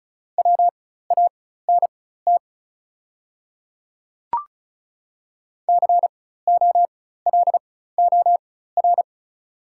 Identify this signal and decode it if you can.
Morse code, heard as a keyed tone, about 36 wpm, spaced as for 16 wpm, 690 Hz. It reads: WANT COLOR
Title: 4x Character Spacing - Top 500 Words - 35wpm